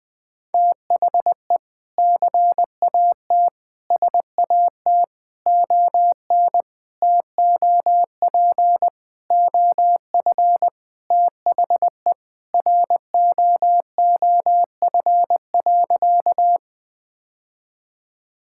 Morse code, heard as a keyed tone, about 21 wpm, 700 Hz. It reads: THE CAT SAT ON TOP OF THE ROOF.
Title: Sentences with 3-Word Phrases - Spoken Beginning - 20wpm